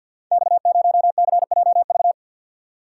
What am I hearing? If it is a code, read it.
Morse code, heard as a keyed tone, about 46 wpm, 700 Hz. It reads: X0CJV
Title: Code Groups - 5 Letter:Numbers - 50wpm